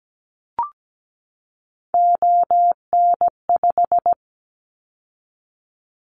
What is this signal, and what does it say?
Morse code, heard as a keyed tone, about 17 wpm, 700 Hz. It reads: ON5
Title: Callsign Prefixes for Top 100 DXCC - 17wpm